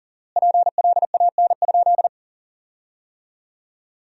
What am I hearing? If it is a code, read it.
Morse code, heard as a keyed tone, about 40 wpm, 700 Hz. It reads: PLAN?
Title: 4x Word-Spacing - Sentences from Top 500 Words - 40wpm